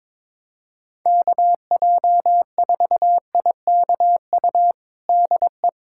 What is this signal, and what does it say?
Morse code, heard as a keyed tone, about 22 wpm, 700 Hz. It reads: KJ4IKU DE
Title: USA Callsign DE Callsign - 6 Characters - 22wpm